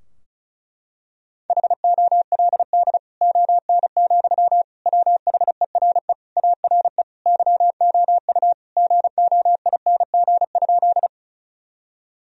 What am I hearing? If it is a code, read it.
Morse code, heard as a keyed tone, about 34 wpm, 700 Hz. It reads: HOLD ON, WHERE ARE YOU GOING?
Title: Sentences from Top 1000 Words - Review Part 2 - 35wpm